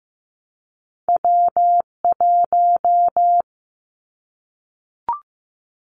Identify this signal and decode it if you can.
Morse code, heard as a keyed tone, about 15 wpm, 700 Hz. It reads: W1
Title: Callsign Prefixes for Top 25 DXCC - Repeated 3x - 15wpm